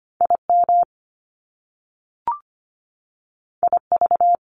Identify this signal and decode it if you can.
Morse code, heard as a keyed tone, about 25 wpm, 700 Hz. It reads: IM IV
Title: Top Two Letter Combinations in Words 25wpm